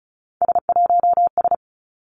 Morse code, heard as a keyed tone, about 34 wpm, 700 Hz. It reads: S1S
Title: Three Challenging Letter-Number Combinations 35wpm